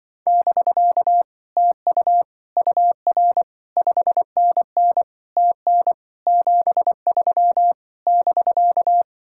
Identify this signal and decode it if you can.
Morse code, heard as a keyed tone, about 25 wpm, 700 Hz. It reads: <BK> TU UR 5NN TN 73 <BK>
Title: Speed-Racing Common QSO Phrases Encore - 20wpm